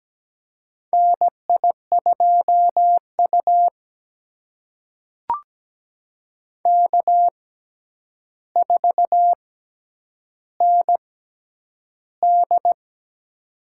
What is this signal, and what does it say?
Morse code, heard as a keyed tone, about 17 wpm, spaced as for 9 wpm, 700 Hz. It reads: NI2U K4ND
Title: USA Call Signs - 4 Characters - 6x Character Spacing - 17wpm